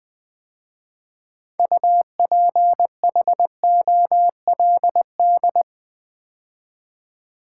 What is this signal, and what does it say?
Morse code, heard as a keyed tone, about 21 wpm, 700 Hz. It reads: UPHOLD
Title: 6 Letter Words - First 3 Letters Given - 20wpm